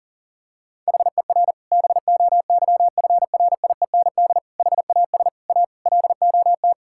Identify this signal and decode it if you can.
Morse code, heard as a keyed tone, about 37 wpm, 700 Hz. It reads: HER BOYFRIEND HAS A LOT
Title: Sentences from Top 1000 Words - Review Part 5 - 40wpm